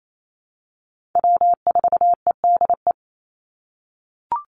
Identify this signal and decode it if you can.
Morse code, heard as a keyed tone, about 28 wpm, 700 Hz. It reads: W4EDE
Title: Call Signs - 28wpm